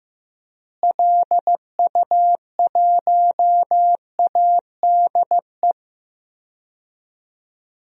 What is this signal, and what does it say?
Morse code, heard as a keyed tone, about 16 wpm, 700 Hz. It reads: LU1ADE
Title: DXCC-100 Call Signs - 6 Characters - Repeated 2x - 15wpm